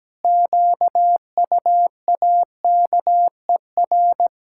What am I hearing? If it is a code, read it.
Morse code, heard as a keyed tone, about 17 wpm, 700 Hz. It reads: QUAKER
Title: Pangram Challenge v2 - 17wpm